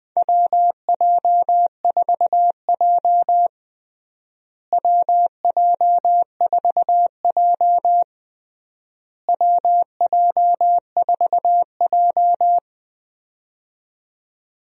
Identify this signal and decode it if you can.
Morse code, heard as a keyed tone, about 20 wpm, 700 Hz. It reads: WJ4J WJ4J WJ4J
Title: USA Call Signs - 4 Characters - Repeated 3x - 20wpm